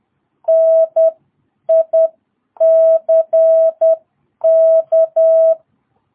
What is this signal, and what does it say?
Morse code, heard as a keyed tone, about 10 wpm, spaced as for 8 wpm, 660 Hz. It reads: NICK
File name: jh8cbh / 2022cw